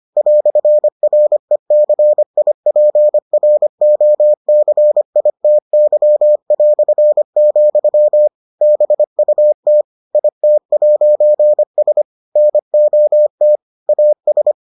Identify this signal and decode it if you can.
Morse code, heard as a keyed tone, about 25 wpm, 600 Hz. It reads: "RECIPROCITY", BUT IT'S NOT AS